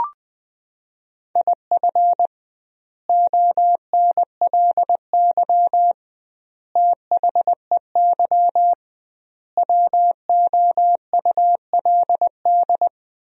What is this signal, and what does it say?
Morse code, heard as a keyed tone, about 20 wpm, 700 Hz. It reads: IF ONLY THEY WOULD
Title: 2x Word-Spacing - Sentences from Top 500 Words - 20wpm